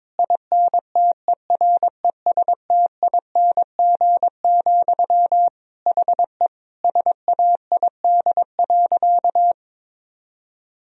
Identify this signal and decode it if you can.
Morse code, heard as a keyed tone, about 22 wpm, 700 Hz. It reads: INTERESTING, HE SAID.
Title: Sentences from Top 1000 Words - Review Part 3 - 22wpm